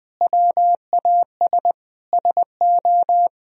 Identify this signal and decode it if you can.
Morse code, heard as a keyed tone, about 20 wpm, 700 Hz. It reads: WAS SO
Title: Sentences from Top 1000 Words - Review Part 4 - 20wpm